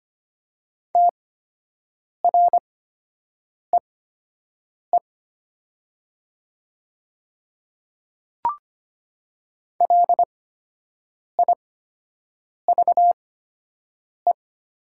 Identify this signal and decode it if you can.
Morse code, heard as a keyed tone, about 25 wpm, spaced as for 7 wpm, 700 Hz. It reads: TREE LIVE
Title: 8x Character Spacing - Top 300 Words - 25wpm